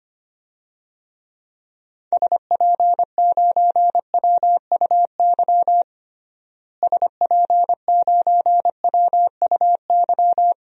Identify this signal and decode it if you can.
Morse code, heard as a keyed tone, about 25 wpm, 700 Hz. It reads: SP9WUY SP9WUY
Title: DXCC-100 Call Signs - 6 Characters - Repeated 2x - 25wpm